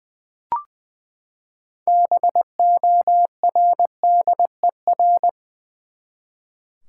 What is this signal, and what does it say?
Morse code, heard as a keyed tone, about 21 wpm, 700 Hz. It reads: BORDER
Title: Six Letter Words - 20wpm